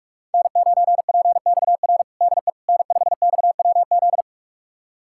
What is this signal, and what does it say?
Morse code, heard as a keyed tone, about 46 wpm, 700 Hz. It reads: N9PXR DE N5XWZ
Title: USA Callsign DE Callsign - 5 Characters - 45wpm